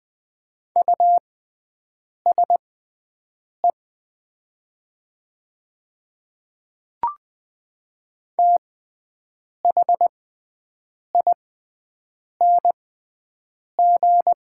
Morse code, heard as a keyed tone, about 20 wpm, spaced as for 7 wpm, 700 Hz. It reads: USE THING
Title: 6x Character Spacing - Top 200 Words - 20wpm